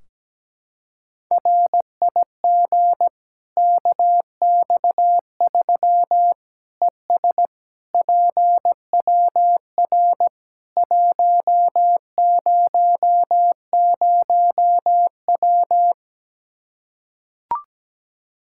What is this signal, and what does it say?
Morse code, heard as a keyed tone, about 17 wpm, 700 Hz. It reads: RIG KX3 ES PWR 100W
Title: Practice QSO - Exchange 2 - Info - 17wpm